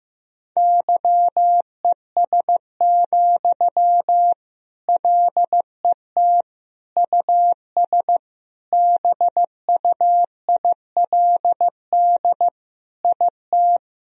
Morse code, heard as a keyed tone, about 15 wpm, 700 Hz. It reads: YES, LET US BUILD IT
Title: Sentences from Top 1000 Words - Review Part 1 - 15wpm